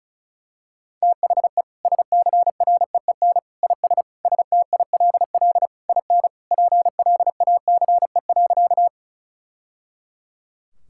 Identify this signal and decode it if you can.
Morse code, heard as a keyed tone, about 36 wpm, 700 Hz. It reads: THE SCREEN IS STILL IN PLACE.
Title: Sentences from Top 1000 Words - Review Part 6 - 35wpm